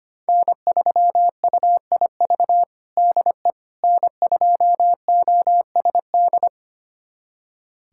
Morse code, heard as a keyed tone, about 25 wpm, 700 Hz. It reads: N3UIV DE N2OSD
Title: USA Callsign DE Callsign - 5 Characters - 25wpm